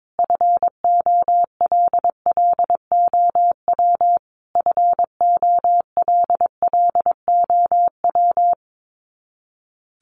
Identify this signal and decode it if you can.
Morse code, heard as a keyed tone, about 22 wpm, 700 Hz. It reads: FOLLOW FOLLOW
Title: Top 500 Words Repeated 22wpm